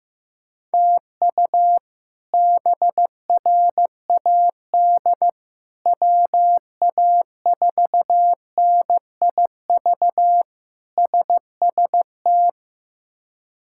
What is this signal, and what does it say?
Morse code, heard as a keyed tone, about 16 wpm, 700 Hz. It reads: TU BRAD WA4NIV SST
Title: K1USN SST QSO Practice - 15wpm